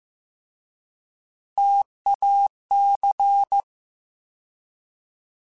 Morse code, heard as a keyed tone, about 15 wpm, 800 Hz. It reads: TAC
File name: T Prefixes / T mp3s